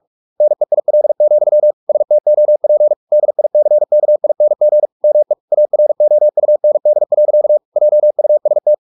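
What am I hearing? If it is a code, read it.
Morse code, heard as a keyed tone, about 44 wpm, 600 Hz. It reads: NEIL, STOP DICKING ME AROUND. JUST
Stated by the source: Morse practice at 45 wpm, the novel Never by Ken Follett in English, from CW med Gnister.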